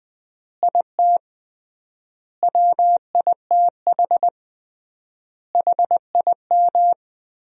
Morse code, heard as a keyed tone, about 21 wpm, 700 Hz. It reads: IT WITH HIM
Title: 3x Word-Spacing - Sentences from Top 500 Words - 20wpm